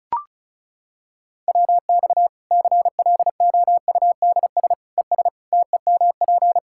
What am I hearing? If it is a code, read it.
Morse code, heard as a keyed tone, about 34 wpm, 700 Hz. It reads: WX CLOUDS ES TEMP
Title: Practice QSO - Exchange 2 - Info - 35wpm